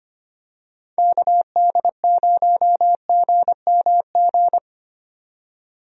Morse code, heard as a keyed tone, about 25 wpm, 700 Hz. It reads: KD0GMG
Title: USA Call Signs - 6 Characters - 25wpm